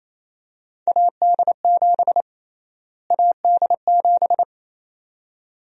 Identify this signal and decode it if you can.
Morse code, heard as a keyed tone, about 27 wpm, 700 Hz. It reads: AD7 AD7
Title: USA Call Sign Prefixes - Repeated 2x - 28wpm